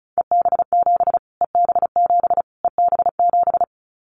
Morse code, heard as a keyed tone, about 34 wpm, 700 Hz. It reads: EB7 EB7 EB7
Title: Callsign Prefixes for Top 25 DXCC - Repeated 3x - 35wpm